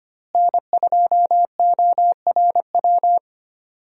Morse code, heard as a keyed tone, about 25 wpm, 700 Hz. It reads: N2ORW
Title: USA Call Signs - 5 Characters - Repeated 3x - 25wpm